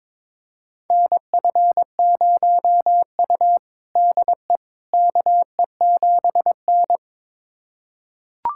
Morse code, heard as a keyed tone, about 21 wpm, 700 Hz. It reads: NF0U DE KE7N
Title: USA Callsign DE Callsign - 4 Characters - 22wpm